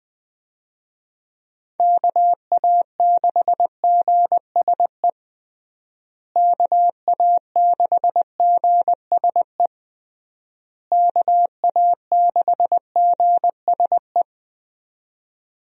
Morse code, heard as a keyed tone, about 20 wpm, 700 Hz. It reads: KA6GSE KA6GSE KA6GSE
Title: USA Call Signs - 6 Characters - Repeated 3x - 20wpm